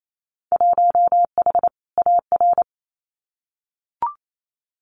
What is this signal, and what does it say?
Morse code, heard as a keyed tone, about 28 wpm, 700 Hz. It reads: 1H AR
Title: Winter Field Day Exchanges Encore 28wpm